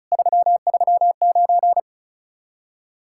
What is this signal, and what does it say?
Morse code, heard as a keyed tone, about 46 wpm, 700 Hz. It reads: 339